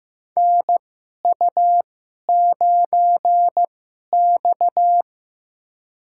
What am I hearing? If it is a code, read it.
Morse code, heard as a keyed tone, about 15 wpm, spaced as for 11 wpm, 700 Hz. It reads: NU9X
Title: USA Call Signs - 4 Characters - 2x Character Spacing - 15wpm